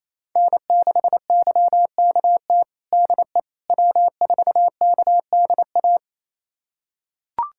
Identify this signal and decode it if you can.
Morse code, heard as a keyed tone, about 27 wpm, 700 Hz. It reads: N6YKT DE W4KDA
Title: USA Callsign DE Callsign - 5 Characters - 28wpm